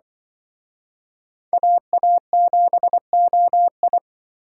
Morse code, heard as a keyed tone, about 24 wpm, 700 Hz. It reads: AA7OI
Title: Speed Racing - USA 2x2 Call Signs 20wpm